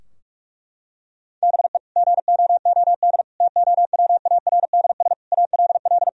Morse code, heard as a keyed tone, about 45 wpm, 700 Hz. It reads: BE GOOD TOWARDS ALL
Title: Sentences from Top 1000 Words - Review Part 2 - 45wpm